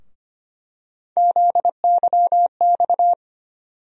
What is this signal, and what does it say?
Morse code, heard as a keyed tone, about 25 wpm, 700 Hz. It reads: ZYX
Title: USA Call Sign Postfixes - 3 Letters 25wpm